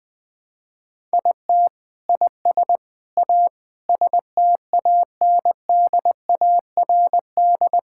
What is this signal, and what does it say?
Morse code, heard as a keyed tone, about 21 wpm, 700 Hz. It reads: IT IS A STANDARD
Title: Sentences from Top 1000 Words - Review Part 6 - 20wpm